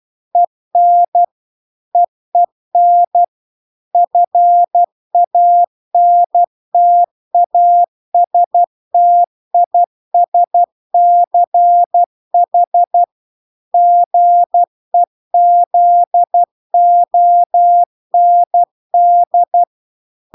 Morse code, heard as a keyed tone, about 12 wpm, 700 Hz. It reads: EN EEN FANTASTISCH GEZOND